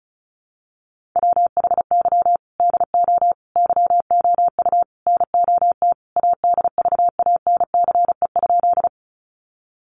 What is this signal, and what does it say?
Morse code, heard as a keyed tone, about 34 wpm, 700 Hz. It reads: WHY DO YOU NOT ADVANCE?
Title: Sentences from Top 1400 Words 35wpm